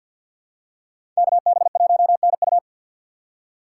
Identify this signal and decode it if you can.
Morse code, heard as a keyed tone, about 50 wpm, 700 Hz. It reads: KB1NU
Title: USA Call Signs - 5 Characters - 4x Character Spacing - 50wpm